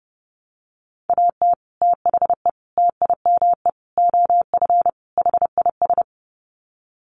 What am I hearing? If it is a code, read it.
Morse code, heard as a keyed tone, about 30 wpm, 700 Hz. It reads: AT THE TIME OF HIS
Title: Sets of 5 Words Encore2 30wpm